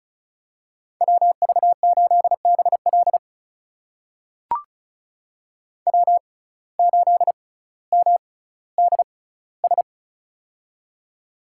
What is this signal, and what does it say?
Morse code, heard as a keyed tone, about 36 wpm, spaced as for 18 wpm, 700 Hz. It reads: WV8BL W8MDS
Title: USA Call Signs - 5 Characters - 6x Character Spacing - 35wpm